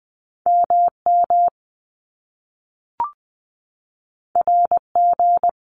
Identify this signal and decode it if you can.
Morse code, heard as a keyed tone, about 20 wpm, 700 Hz. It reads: MM RG
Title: Top Two Letter Combinations in Words 20wpm